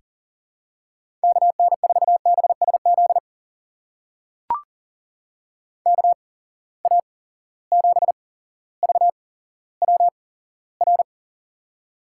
Morse code, heard as a keyed tone, about 40 wpm, spaced as for 17 wpm, 700 Hz. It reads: KN4BSZ KA7VWR
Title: USA Call Signs - 6 Characters - 8x Character Spacing - 40wpm